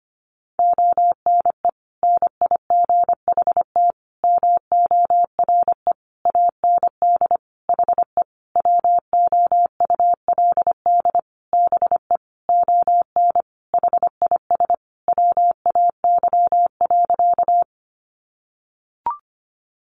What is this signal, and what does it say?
Morse code, heard as a keyed tone, about 25 wpm, 700 Hz. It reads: ONE NIGHT MORE AND HE WOULD BE ON HIS WAY.
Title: Sentences from Top 300 Words 25wpm